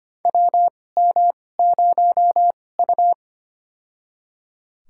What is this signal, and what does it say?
Morse code, heard as a keyed tone, about 25 wpm, spaced as for 18 wpm, 700 Hz. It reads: WM0U
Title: USA Call Signs - 4 Characters - 2x Character Spacing - 25wpm